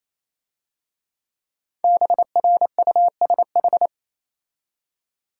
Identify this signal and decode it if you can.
Morse code, heard as a keyed tone, about 25 wpm, 700 Hz. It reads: BRUSH
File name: Easy Words - Speed-Racing with Hints - 22wpm